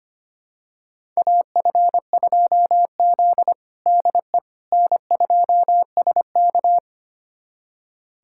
Morse code, heard as a keyed tone, about 25 wpm, 700 Hz. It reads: AF2Z DE N2SK
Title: USA Callsign DE Callsign - 4 Characters - 25wpm